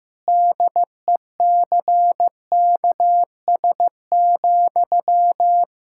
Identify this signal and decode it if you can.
Morse code, heard as a keyed tone, about 15 wpm, 700 Hz. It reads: DECKS,
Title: Pangram Challenge v2 - 15wpm